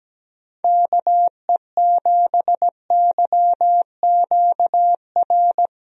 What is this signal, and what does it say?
Morse code, heard as a keyed tone, about 17 wpm, 700 Hz. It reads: KE7YQR
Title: USA Call Signs - 6 Characters - Repeated 2x - 17wpm